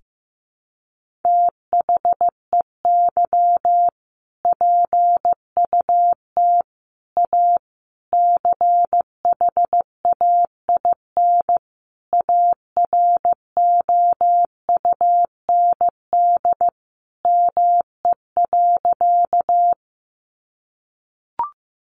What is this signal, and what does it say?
Morse code, heard as a keyed tone, about 15 wpm, 700 Hz. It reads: THEY PUT A CHAIN AROUND ME.
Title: Sentences from Top 1500 Words 15wpm